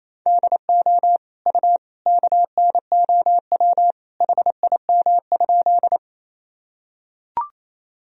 Morse code, heard as a keyed tone, about 28 wpm, 700 Hz. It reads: DO U KNOW HIM?
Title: Sentences from Top 100 Words and Abbreviations 28wpm